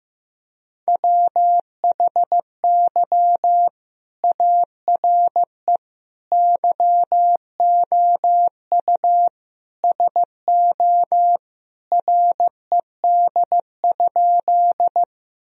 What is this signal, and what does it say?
Morse code, heard as a keyed tone, about 15 wpm, 700 Hz. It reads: WHY ARE YOU SO RED?
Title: Sentences from Top 1000 Words - Review Part 2 - 15wpm